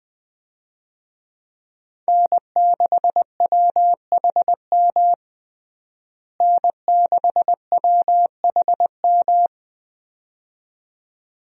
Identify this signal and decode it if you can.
Morse code, heard as a keyed tone, about 21 wpm, 700 Hz. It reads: N6WHM N6WHM
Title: USA Call Signs - 5 Characters - Repeated 2x - 20wpm